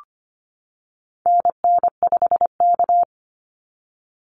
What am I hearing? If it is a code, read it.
Morse code, heard as a keyed tone, about 25 wpm, 700 Hz. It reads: NN5K
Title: NA SOTA Activator Callsigns 25wpm